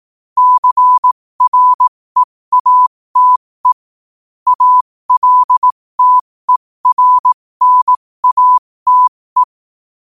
Morse code, heard as a keyed tone, about 18 wpm, spaced as for 15 wpm, 1000 Hz. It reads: CREATE ALTERNATE
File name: qotdfiles / 15wpm